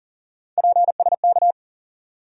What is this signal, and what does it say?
Morse code, heard as a keyed tone, about 40 wpm, 700 Hz. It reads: PSK
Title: USA Call Sign Postfixes - 3 Letters 40wpm